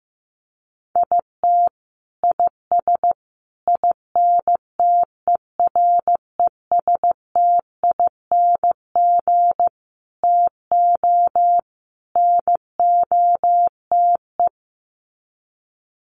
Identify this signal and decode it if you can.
Morse code, heard as a keyed tone, about 15 wpm, 700 Hz. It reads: IT IS INTERESTING TO NOTE
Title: Sets of 5 Words Encore2 15wpm